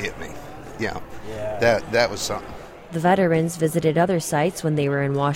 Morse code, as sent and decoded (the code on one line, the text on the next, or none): none